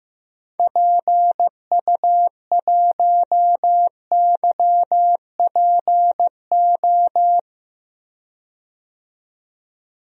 Morse code, heard as a keyed tone, about 16 wpm, 700 Hz. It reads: PU1YPO
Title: DXCC-100 Call Signs - 6 Characters - 15wpm